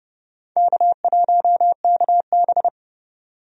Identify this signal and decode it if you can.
Morse code, heard as a keyed tone, about 30 wpm, 700 Hz. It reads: K1KB
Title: USA Call Signs - 4 Characters - 30wpm